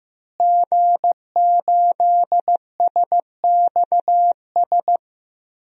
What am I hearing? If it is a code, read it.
Morse code, heard as a keyed tone, about 15 wpm, 700 Hz. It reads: G8SXS
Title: DXCC-100 Call Signs - 5 Characters - Repeated 3x - 15wpm